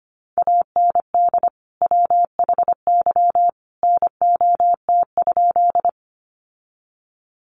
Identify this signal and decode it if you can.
Morse code, heard as a keyed tone, about 25 wpm, 700 Hz. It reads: AND WHY NOT?
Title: Sentences from Top 600 Words 25wpm